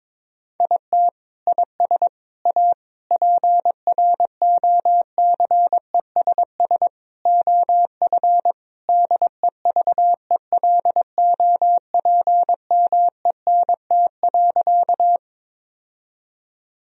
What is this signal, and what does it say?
Morse code, heard as a keyed tone, about 22 wpm, 700 Hz. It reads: IT IS A PROCESS OF DEVELOPMENT.
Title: Sentences from Top 1000 Words - Review Part 4 - 22wpm